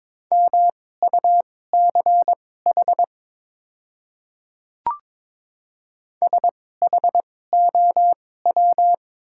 Morse code, heard as a keyed tone, about 22 wpm, spaced as for 16 wpm, 700 Hz. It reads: MUCH SHOW
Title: 2x Character Spacing - Top 300 Words - 22wpm